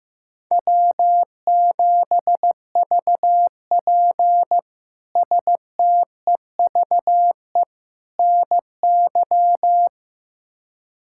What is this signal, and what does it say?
Morse code, heard as a keyed tone, about 15 wpm, 700 Hz. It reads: W7VP STEVE NY
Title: K1USN SST QSO Practice - 15wpm